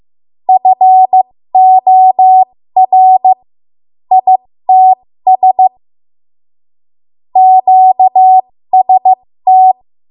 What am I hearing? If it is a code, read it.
Morse code, heard as a keyed tone, about 15 wpm, spaced as for 13 wpm, 750 Hz. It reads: FOR ITS QST
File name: Morse / 13WPM